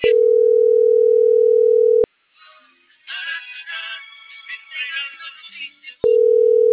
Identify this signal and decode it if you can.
Ringback tone (North America), 440 + 480 Hz.